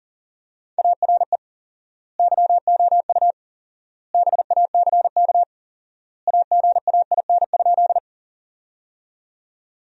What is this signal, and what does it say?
Morse code, heard as a keyed tone, about 40 wpm, 700 Hz. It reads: ARE YOU BACK AGAIN?
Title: 4x Word-Spacing - Sentences from Top 200 Words - 40wpm